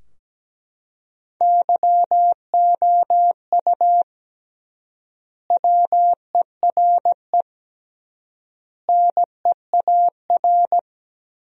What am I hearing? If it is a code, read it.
Morse code, heard as a keyed tone, about 17 wpm, 700 Hz. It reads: YOU WERE NEAR
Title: 3x Word-Spacing - Sentences from Top 200 Words - 17wpm